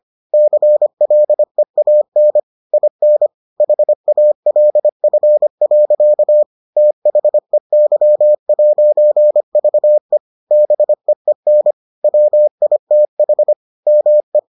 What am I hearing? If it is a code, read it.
Morse code, heard as a keyed tone, about 25 wpm, 600 Hz. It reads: CLEAN IN HALF. THEY'VE BEEN WITH ME